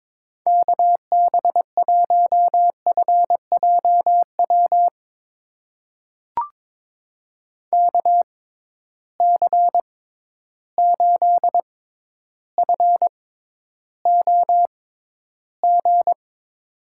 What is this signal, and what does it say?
Morse code, heard as a keyed tone, about 22 wpm, spaced as for 11 wpm, 700 Hz. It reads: KB1FJW KC8FOG
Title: USA Call Signs - 6 Characters - 6x Character Spacing - 22wpm